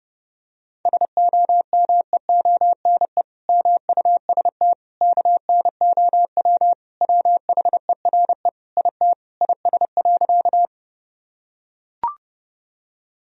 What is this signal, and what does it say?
Morse code, heard as a keyed tone, about 30 wpm, 700 Hz. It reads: SOMEONE MUST KNOW WHERE IT IS.
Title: Sentences from Top 1000 Words - Review Part 4 - 30wpm